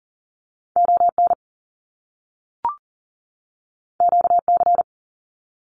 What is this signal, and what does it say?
Morse code, heard as a keyed tone, about 40 wpm, 700 Hz. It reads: ON QC
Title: Canadian Provinces and Territories 40wpm